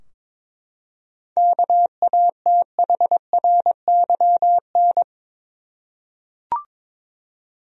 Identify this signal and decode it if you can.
Morse code, heard as a keyed tone, about 21 wpm, 700 Hz. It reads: KATHRYN
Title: USA Common Names - 22wpm